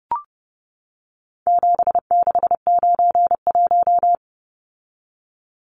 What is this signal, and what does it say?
Morse code, heard as a keyed tone, about 30 wpm, 700 Hz. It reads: 7691
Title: Numbers - 4-Digit 30wpm